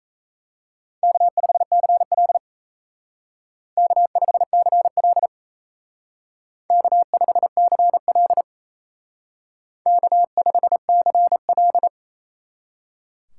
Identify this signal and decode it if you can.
Morse code, heard as a keyed tone, about 35 wpm, 700 Hz. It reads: K5CL K5CL K5CL K5CL